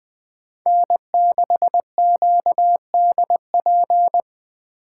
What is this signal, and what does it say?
Morse code, heard as a keyed tone, about 20 wpm, 700 Hz. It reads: N6QDP
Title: Speed Racing - USA 1x3 Call Signs 17wpm